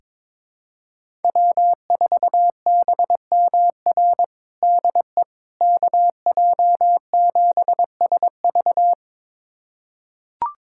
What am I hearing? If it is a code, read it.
Morse code, heard as a keyed tone, about 22 wpm, 700 Hz. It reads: W4BMR DE KJ7SV
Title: USA Callsign DE Callsign - 5 Characters - 22wpm